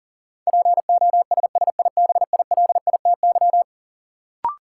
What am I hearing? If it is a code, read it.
Morse code, heard as a keyed tone, about 40 wpm, 700 Hz. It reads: POSSIBILITY
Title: Sets of 3 Words - New Word Drill - 40wpm